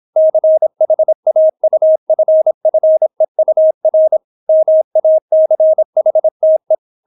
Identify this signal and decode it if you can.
Morse code, heard as a keyed tone, about 26 wpm, 630 Hz. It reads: CHAUFFEUR MACHTE